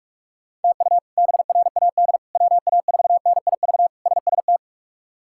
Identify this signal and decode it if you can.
Morse code, heard as a keyed tone, about 41 wpm, 700 Hz. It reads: TU BRAD WA4NIV SST
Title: K1USN SST QSO Practice - 45wpm